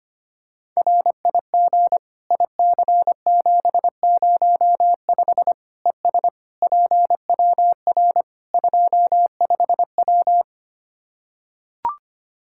Morse code, heard as a keyed tone, about 25 wpm, 700 Hz. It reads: RIG IC705 ES PWR 25W
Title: Practice QSO - Exchange 2 - Info - 25wpm